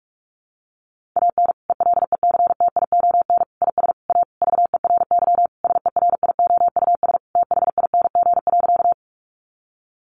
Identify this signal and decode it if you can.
Morse code, heard as a keyed tone, about 44 wpm, 700 Hz. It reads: AN ELECTION IS A VERY SERIOUS THING.
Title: Sentences from Top 1500 Words 45wpm